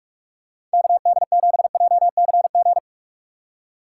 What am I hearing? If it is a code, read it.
Morse code, heard as a keyed tone, about 42 wpm, 700 Hz. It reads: KD7JCG
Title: USA Call Signs - 2x Character Spacing - 45wpm